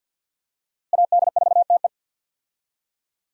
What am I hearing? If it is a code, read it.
Morse code, heard as a keyed tone, about 52 wpm, 700 Hz. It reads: AD4TE
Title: USA Call Signs - 5 Characters - Repeated 2x - 50wpm